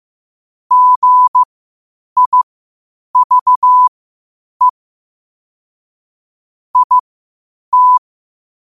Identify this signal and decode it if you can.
Morse code, heard as a keyed tone, about 15 wpm, spaced as for 8 wpm, 1000 Hz. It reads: GIVE IT
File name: qotdfiles / 8wpm